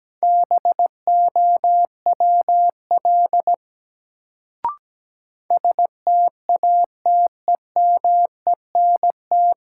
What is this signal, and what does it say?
Morse code, heard as a keyed tone, about 17 wpm, 700 Hz. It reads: BOWL STATEMENT
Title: Sentences from Top 1200 Words - New Word Drill - 17wpm